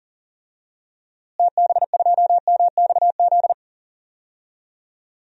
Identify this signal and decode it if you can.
Morse code, heard as a keyed tone, about 40 wpm, 700 Hz. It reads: TB2MXZ
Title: DXCC-100 Call Signs - 6 Characters - Repeated 2x - 40wpm